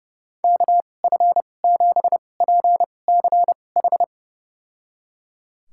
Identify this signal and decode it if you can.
Morse code, heard as a keyed tone, about 30 wpm, spaced as for 22 wpm, 700 Hz. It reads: KF7PCH
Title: USA Call Signs - 6 Characters - 2x Character Spacing - 30wpm